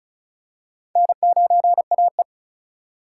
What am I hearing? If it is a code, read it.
Morse code, heard as a keyed tone, about 36 wpm, 700 Hz. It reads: N9AE
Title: USA Call Signs - 4 Characters - 4x Character Spacing - 35wpm